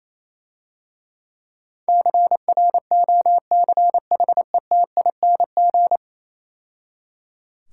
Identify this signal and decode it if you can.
Morse code, heard as a keyed tone, about 27 wpm, 700 Hz. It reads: CROCHETING
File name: Easy Words - Speed-Racing with Hints - 22wpm